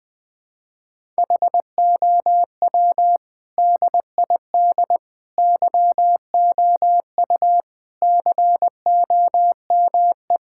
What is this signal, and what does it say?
Morse code, heard as a keyed tone, about 20 wpm, 700 Hz. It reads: HOW DID YOU COME
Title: Sets of 8 Words - 20wpm